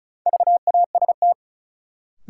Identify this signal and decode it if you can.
Morse code, heard as a keyed tone, about 36 wpm, 700 Hz. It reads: VAST
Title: Sets of 4 Words - New Word Drill - 35wpm